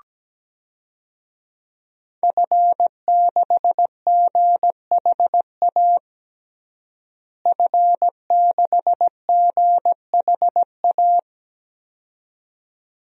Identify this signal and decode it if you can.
Morse code, heard as a keyed tone, about 17 wpm, 700 Hz. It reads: F6GHA F6GHA
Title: DXCC-100 Call Signs - 5 Characters - Repeated 2x - 17wpm